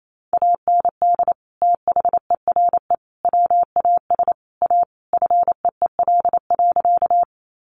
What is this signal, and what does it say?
Morse code, heard as a keyed tone, about 28 wpm, 700 Hz. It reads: AND THERE WAS A FEEL.
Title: A Sound Of Thunder 28wpm